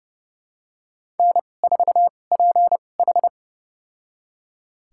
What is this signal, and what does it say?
Morse code, heard as a keyed tone, about 29 wpm, spaced as for 21 wpm, 700 Hz. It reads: N4PH
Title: USA Call Signs - 4 Characters - 2x Character Spacing - 30wpm